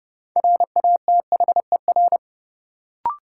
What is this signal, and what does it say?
Morse code, heard as a keyed tone, about 29 wpm, 700 Hz. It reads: RATHER
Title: Sentences from Top 700 Words - New Word Drill - 30wpm